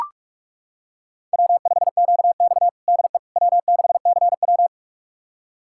Morse code, heard as a keyed tone, about 44 wpm, 690 Hz. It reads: W5QX DE W6CW
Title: USA Callsign DE Callsign - 4 Characters - 45wpm